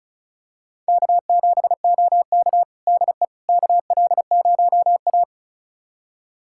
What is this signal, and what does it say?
Morse code, heard as a keyed tone, about 34 wpm, 700 Hz. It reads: K7OK DE KL0A
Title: USA Callsign DE Callsign - 4 Characters - 35wpm